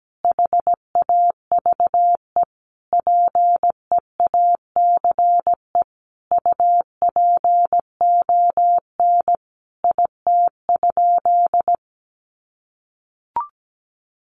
Morse code, heard as a keyed tone, about 17 wpm, 700 Hz. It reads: HAVE PEACE UPON IT?
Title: Sentences from Top 1500 Words 17wpm